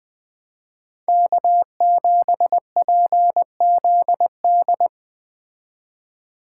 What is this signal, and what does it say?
Morse code, heard as a keyed tone, about 20 wpm, 700 Hz. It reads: K7PZD
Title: Speed Racing - USA 1x3 Call Signs 17wpm